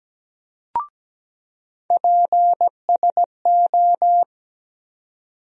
Morse code, heard as a keyed tone, about 17 wpm, 700 Hz. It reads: PSO
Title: USA Call Sign Postfixes - 3 Letters 17wpm